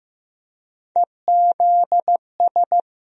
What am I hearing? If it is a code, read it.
Morse code, heard as a keyed tone, about 15 wpm, 700 Hz. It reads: EZS
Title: USA Call Sign Postfixes - 3 Letters 15wpm